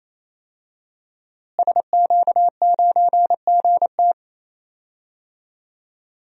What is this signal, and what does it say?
Morse code, heard as a keyed tone, about 28 wpm, 700 Hz. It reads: SQ9GT
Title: DXCC-100 Call Signs - 5 Characters - Repeated 3x - 28wpm